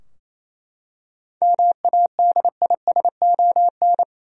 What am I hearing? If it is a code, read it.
Morse code, heard as a keyed tone, about 28 wpm, 700 Hz. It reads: MADISON
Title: USA Common Names - 28wpm